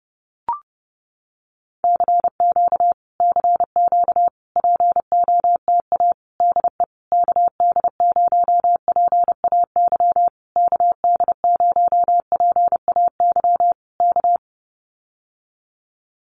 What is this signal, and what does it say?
Morse code, heard as a keyed tone, about 30 wpm, 700 Hz. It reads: CQ CQ POTA DE KD0PAY KD0PAY K
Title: Common QSO Phrases Encore 30wpm